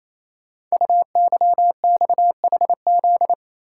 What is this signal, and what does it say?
Morse code, heard as a keyed tone, about 29 wpm, 700 Hz. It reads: UYXHZ
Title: Code Groups - 5 Letters - 28wpm